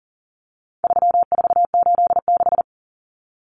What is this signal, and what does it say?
Morse code, heard as a keyed tone, about 40 wpm, 700 Hz. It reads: 3486
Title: Common Cut Numbers - 4-Digit 40wpm